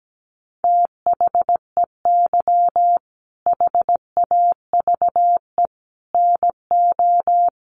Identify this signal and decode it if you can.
Morse code, heard as a keyed tone, about 17 wpm, 700 Hz. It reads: THEY HAVE NO
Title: Sentences from Top 1300 Words 17wpm